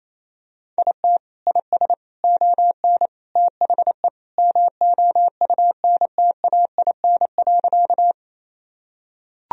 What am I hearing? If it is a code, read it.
Morse code, heard as a keyed tone, about 28 wpm, 700 Hz. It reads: IT IS ON THE MOUNTAIN.
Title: Sentences from Top 1000 Words - Review Part 1 - 28wpm